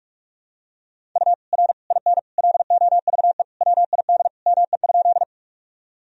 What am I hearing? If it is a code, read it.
Morse code, heard as a keyed tone, about 47 wpm, 700 Hz. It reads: U R IN LOVE WID ME?
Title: Sentences from Top 400 Words and Abbreviations 45wpm